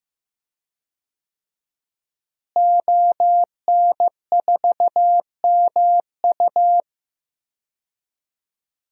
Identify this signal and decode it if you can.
Morse code, heard as a keyed tone, about 15 wpm, 700 Hz. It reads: ON4MU
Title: DXCC-100 Call Signs - 5 Characters - Repeated 3x - 15wpm